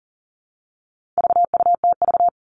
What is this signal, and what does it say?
Morse code, heard as a keyed tone, about 40 wpm, 700 Hz. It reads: VUTV
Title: All Cut Numbers - 4-Digit 40wpm